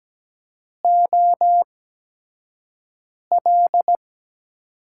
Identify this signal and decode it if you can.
Morse code, heard as a keyed tone, about 17 wpm, spaced as for 5 wpm, 700 Hz. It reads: OL
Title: USA Call Signs - 5 Characters - 8x Character Spacing - 17wpm